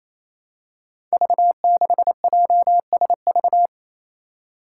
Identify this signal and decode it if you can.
Morse code, heard as a keyed tone, about 28 wpm, 700 Hz. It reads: V6JSV
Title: Five Challenging Letter-Number Combinations 28wpm